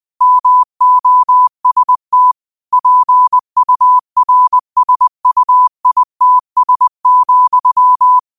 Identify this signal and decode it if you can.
Morse code, heard as a keyed tone, about 20 wpm, 1000 Hz. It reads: MOST PURSUITS,
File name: qotdfiles / 20wpm